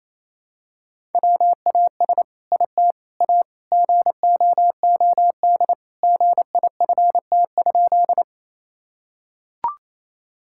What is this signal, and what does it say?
Morse code, heard as a keyed tone, about 28 wpm, 700 Hz. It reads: WAS IT A GOOD GIFT?
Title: Sentences from Top 1000 Words - Review Part 6 - 28wpm